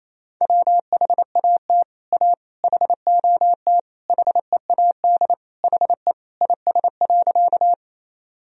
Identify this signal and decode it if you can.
Morse code, heard as a keyed tone, about 28 wpm, 700 Hz. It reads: WHAT A HOT HEAD HE IS.
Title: Sentences from Top 1000 Words - Review Part 3 - 28wpm